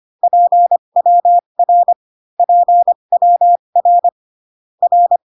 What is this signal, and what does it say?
Morse code, heard as a keyed tone, about 25 wpm, spaced as for 22 wpm, 700 Hz. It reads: PWR PWR R
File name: CW common abbreviations alphabetical repeated 3 times 25 18wpm